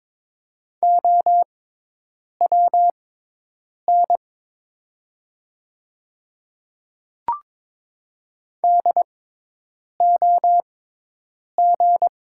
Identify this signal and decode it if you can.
Morse code, heard as a keyed tone, about 22 wpm, spaced as for 8 wpm, 700 Hz. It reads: OWN DOG
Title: 6x Character Spacing - Top 500 Words - 22wpm